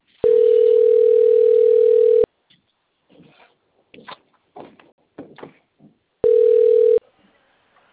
Ringback tone (North America), 440 + 480 Hz.